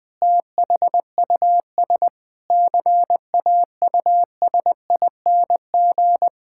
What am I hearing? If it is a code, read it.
Morse code, heard as a keyed tone, about 21 wpm, 700 Hz. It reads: THUS CAUSING
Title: Pangram Challenge v2 - 20wpm